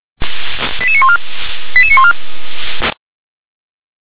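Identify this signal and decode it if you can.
Five-tone selective call (ZVEI), heard as quick runs of five tones, tones of about 70 ms. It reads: 88914 88914
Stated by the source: ZVEI five-tone call 88786 received by SDR.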